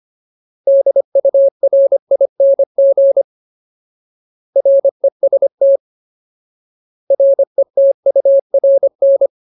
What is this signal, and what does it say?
Morse code, heard as a keyed tone, about 25 wpm, 550 Hz. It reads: DURING REST RETURN
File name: Ultimate Word List / 25WPM Ultimate